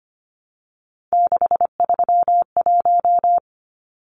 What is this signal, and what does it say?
Morse code, heard as a keyed tone, about 25 wpm, 700 Hz. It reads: 631
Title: Numbers - 3-Digit 25wpm